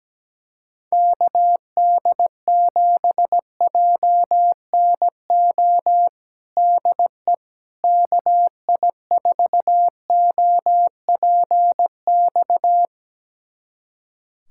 Morse code, heard as a keyed tone, about 17 wpm, 700 Hz. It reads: KD7JNO DE KI4OPX
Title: USA Callsign DE Callsign 17wpm